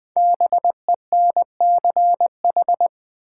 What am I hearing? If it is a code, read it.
Morse code, heard as a keyed tone, about 20 wpm, 700 Hz. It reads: BENCH
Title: Sentences from Top 1400 Words - New Word Drill Alt - 20wpm